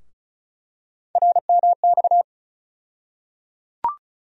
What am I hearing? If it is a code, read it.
Morse code, heard as a keyed tone, about 34 wpm, 700 Hz. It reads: RMX
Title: USA Call Sign Postfixes - 3 Letters 35wpm